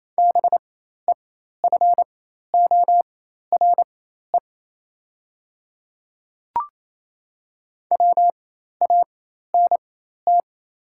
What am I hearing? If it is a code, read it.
Morse code, heard as a keyed tone, about 28 wpm, spaced as for 13 wpm, 700 Hz. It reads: BEFORE WANT
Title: 4x Character Spacing - Top 300 Words - 28wpm